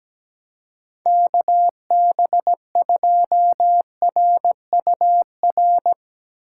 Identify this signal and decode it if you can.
Morse code, heard as a keyed tone, about 17 wpm, 700 Hz. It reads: KB2RUR
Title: USA Call Signs - 3x Character Spacing - 17wpm